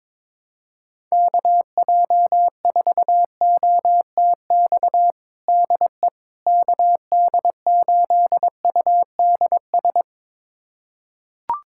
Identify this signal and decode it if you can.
Morse code, heard as a keyed tone, about 22 wpm, 700 Hz. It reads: KJ4OTX DE KD8UDS
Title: USA Callsign DE Callsign - 6 Characters - 22wpm